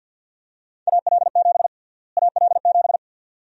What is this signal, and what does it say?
Morse code, heard as a keyed tone, about 51 wpm, 700 Hz. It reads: AL7 AL7
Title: USA Call Sign Prefixes - Repeated 2x - 50wpm